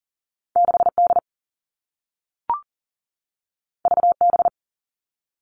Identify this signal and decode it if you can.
Morse code, heard as a keyed tone, about 40 wpm, 700 Hz. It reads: 6D VB
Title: All Cut Numbers - 2-Digit 40wpm